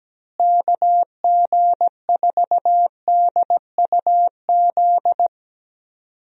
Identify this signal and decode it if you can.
Morse code, heard as a keyed tone, about 17 wpm, 700 Hz. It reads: KG4DUZ
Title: USA Call Signs - 6 Characters - Repeated 3x - 17wpm